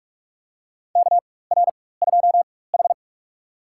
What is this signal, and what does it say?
Morse code, heard as a keyed tone, about 46 wpm, spaced as for 21 wpm, 700 Hz. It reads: KR2H
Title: USA Call Signs - 4 Characters - 4x Character Spacing - 45wpm